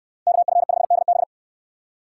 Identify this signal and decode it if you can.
Morse code, heard as a keyed tone, about 69 wpm, 700 Hz. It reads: KF5DB